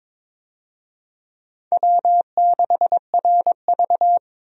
Morse code, heard as a keyed tone, about 21 wpm, 700 Hz. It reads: W6RV
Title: USA Call Signs - 4 Characters - 22wpm